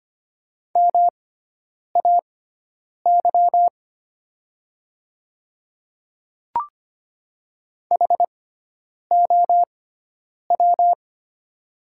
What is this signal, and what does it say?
Morse code, heard as a keyed tone, about 25 wpm, spaced as for 9 wpm, 700 Hz. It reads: MAY HOW
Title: 6x Character Spacing - Top 100 Words - 25wpm